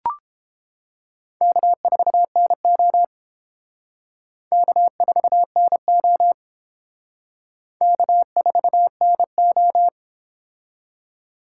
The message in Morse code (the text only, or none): K4NO K4NO K4NO